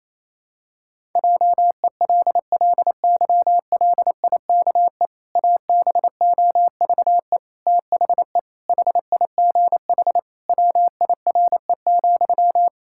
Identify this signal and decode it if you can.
Morse code, heard as a keyed tone, about 28 wpm, 700 Hz. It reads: JELLYLIKE ABOVE THE HIGH WIRE,
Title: Pangram Challenge v2 - 28wpm